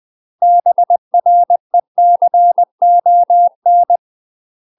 Morse code, heard as a keyed tone, about 20 wpm, 700 Hz. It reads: BRECON
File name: UK MASTER QSOs TRAINING / 20WPM / UK MASTER QSOs 0014